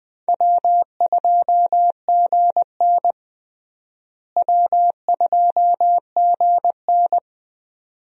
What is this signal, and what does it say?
Morse code, heard as a keyed tone, about 20 wpm, 700 Hz. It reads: W2GN W2GN
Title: USA Call Signs - 4 Characters - Repeated 2x - 20wpm